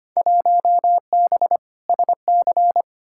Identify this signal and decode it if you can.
Morse code, heard as a keyed tone, about 25 wpm, 700 Hz. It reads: 1B SC
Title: Field Day Exchanges Encore - 25wpm